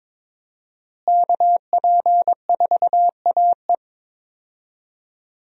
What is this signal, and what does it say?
Morse code, heard as a keyed tone, about 22 wpm, 700 Hz. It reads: KP4AE
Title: USA Call Signs - 5 Characters - Repeated 2x - 22wpm